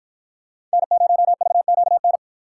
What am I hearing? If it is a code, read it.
Morse code, heard as a keyed tone, about 53 wpm, 700 Hz. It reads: N0VQN